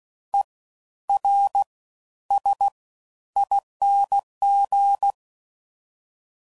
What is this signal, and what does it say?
Morse code, heard as a keyed tone, about 16 wpm, spaced as for 11 wpm, 800 Hz. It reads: ERSING
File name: Suffixes / Special INGs / Fast Ings